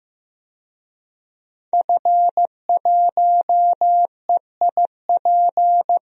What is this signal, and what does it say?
Morse code, heard as a keyed tone, about 15 wpm, 700 Hz. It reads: F1EIP
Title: DXCC-100 Call Signs - 5 Characters - Repeated 3x - 15wpm